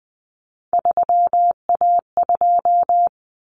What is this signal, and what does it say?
Morse code, heard as a keyed tone, about 20 wpm, 700 Hz. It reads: 3A2
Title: Common Cut Numbers - 3-Digit 20wpm